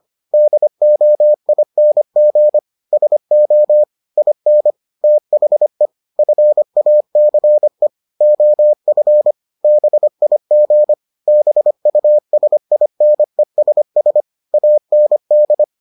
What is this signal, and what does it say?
Morse code, heard as a keyed tone, about 25 wpm, 600 Hz. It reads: DOING SO IN THE FACE OF BIG BUSINESS AND